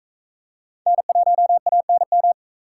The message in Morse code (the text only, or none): N1ANM